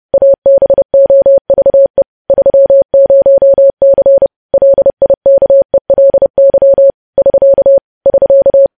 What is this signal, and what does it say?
Morse code, heard as a keyed tone, about 30 wpm, 550 Hz. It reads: ABOVE 30C LIKELY <SK> <SK>
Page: news-headlines-in-morse-code-at-30-wpm